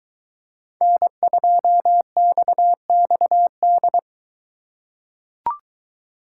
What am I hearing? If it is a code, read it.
Morse code, heard as a keyed tone, about 22 wpm, 700 Hz. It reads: N2XXD